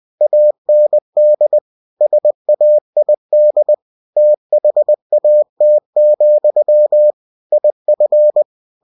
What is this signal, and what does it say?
Morse code, heard as a keyed tone, about 20 wpm, 600 Hz. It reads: AND SAID THAT, IF
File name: cw-books / winnie-the-pooh / winnie-20wpm-page-0035